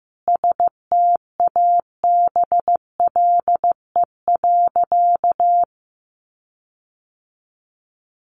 Morse code, heard as a keyed tone, about 15 wpm, 700 Hz. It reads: STABLE.
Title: Sentences from Top 1500 Words 15wpm